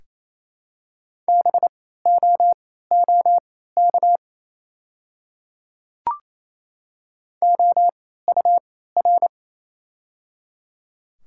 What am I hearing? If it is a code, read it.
Morse code, heard as a keyed tone, about 28 wpm, spaced as for 16 wpm, 700 Hz. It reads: BOOK OUR
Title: 3x Character Spacing - Top 300 Words - 28wpm